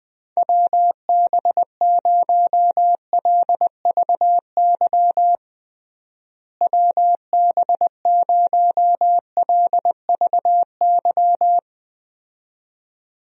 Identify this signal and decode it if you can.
Morse code, heard as a keyed tone, about 21 wpm, 700 Hz. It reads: WB0LVY WB0LVY
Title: USA Call Signs - 6 Characters - Repeated 2x - 20wpm